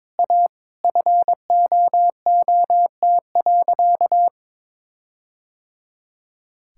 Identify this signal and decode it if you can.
Morse code, heard as a keyed tone, about 22 wpm, 700 Hz. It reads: A FOOT.
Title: Sentences from Top 1600 Words 22wpm